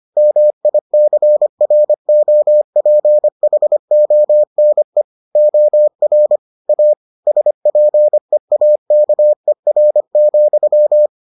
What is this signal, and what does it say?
Morse code, heard as a keyed tone, about 25 wpm, 600 Hz. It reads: MICROPHONE OR A SPEAKER,